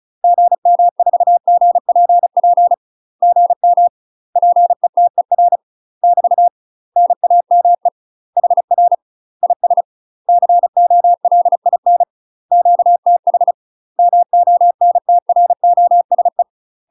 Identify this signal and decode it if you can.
Morse code, heard as a keyed tone, about 34 wpm, 700 Hz. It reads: GM4GPP GM PETER <BT> NAME HR IS COLIN QTH MONTROSE
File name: UK MASTER QSOs TRAINING / 35WPM / UK MASTER QSOs 0043